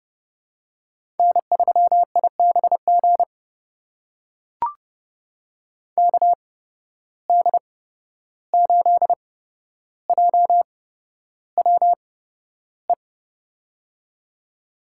Morse code, heard as a keyed tone, about 30 wpm, spaced as for 12 wpm, 700 Hz. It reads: N3IBG KD8JWE
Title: USA Call Signs - 8x Character Spacing - 30wpm